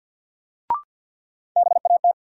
Morse code, heard as a keyed tone, about 52 wpm, 700 Hz. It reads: BAT
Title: Sentences from Top 1500 Words - New Word Drill - 50wpm